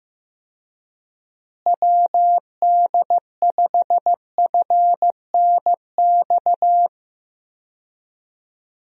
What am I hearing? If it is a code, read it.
Morse code, heard as a keyed tone, about 15 wpm, 700 Hz. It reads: WD5FNX
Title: USA Call Signs - 6 Characters - Repeated 2x - 15wpm